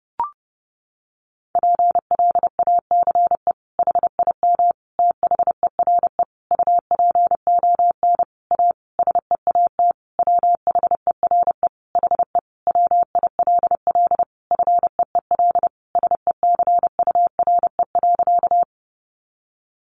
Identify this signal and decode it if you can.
Morse code, heard as a keyed tone, about 30 wpm, 700 Hz. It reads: PLACE HIM THERE UPON A SEAT WHERE HE WILL FEEL SECURE.
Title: Sentences from Top 1500 Words 30wpm